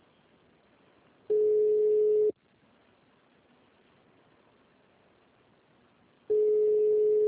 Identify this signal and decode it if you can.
Ringback tone (Europe, 425 Hz).
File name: out- -2012- - - .1853